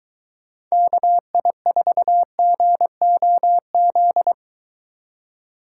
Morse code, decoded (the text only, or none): KI4GOZ